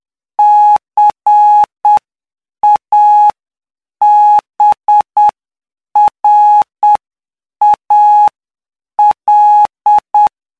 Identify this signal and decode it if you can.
Morse code, heard as a keyed tone, about 8 wpm, spaced as for 7 wpm, 840 Hz. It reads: CABRAL
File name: cw / py2lcd / 14-PY2LCD-Li��o 23